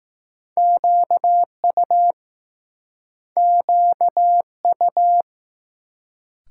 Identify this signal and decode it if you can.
Morse code, heard as a keyed tone, about 20 wpm, 700 Hz. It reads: QU QU